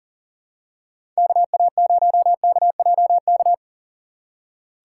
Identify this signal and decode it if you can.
Morse code, heard as a keyed tone, about 40 wpm, 700 Hz. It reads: KA0KJK
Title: USA Call Signs - 6 Characters - 2x Character Spacing - 40wpm